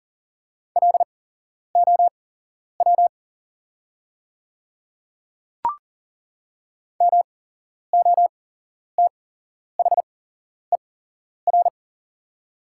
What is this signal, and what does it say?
Morse code, heard as a keyed tone, about 40 wpm, spaced as for 11 wpm, 690 Hz. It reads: LOW MOTHER
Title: 8x Character Spacing - Top 300 Words - 40wpm